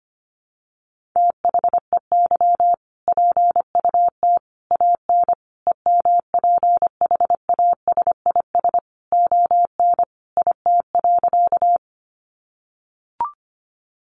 Morse code, heard as a keyed tone, about 25 wpm, 700 Hz. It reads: THEY PUT AN EMPHASIS ON IT.
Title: Sentences from Top 1400 Words 25wpm